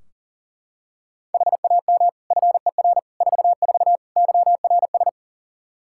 Morse code, heard as a keyed tone, about 40 wpm, 700 Hz. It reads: HAM FER 44 YRS
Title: Practice QSO - Exchange 3 - More Info - 40wpm